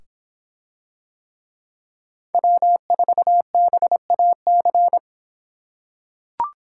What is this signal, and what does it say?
Morse code, heard as a keyed tone, about 24 wpm, 700 Hz. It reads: W4BAC